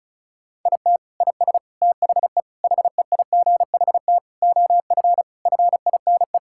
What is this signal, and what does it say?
Morse code, heard as a keyed tone, about 36 wpm, 700 Hz. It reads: IT IS THE HEIGHT OF FINE
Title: Sentences from Top 1600 Words 35wpm